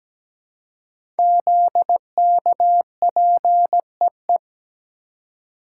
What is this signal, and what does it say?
Morse code, heard as a keyed tone, about 17 wpm, 700 Hz. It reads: ZKPEE
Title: Code Groups - 5 Letters - 17wpm